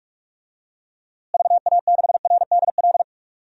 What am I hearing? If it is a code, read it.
Morse code, heard as a keyed tone, about 41 wpm, 700 Hz. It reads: VA6RDL
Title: DXCC-100 Call Signs - 6 Characters - Repeated 3x - 45wpm